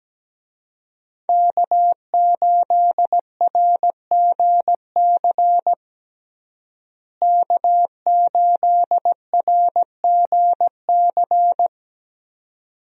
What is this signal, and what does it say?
Morse code, heard as a keyed tone, about 17 wpm, 700 Hz. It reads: K8RGC K8RGC
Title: USA Call Signs - 5 Characters - Repeated 2x - 17wpm